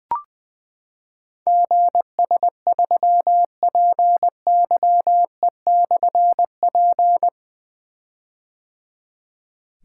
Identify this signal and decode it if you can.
Morse code, heard as a keyed tone, about 21 wpm, 700 Hz. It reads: GS3PYE/P
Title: Difficult Call Signs 20wpm